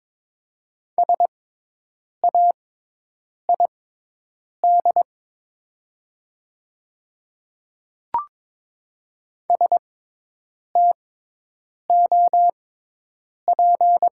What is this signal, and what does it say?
Morse code, heard as a keyed tone, about 22 wpm, spaced as for 8 wpm, 700 Hz. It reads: SAID STOP
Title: 6x Character Spacing - Top 300 Words - 22wpm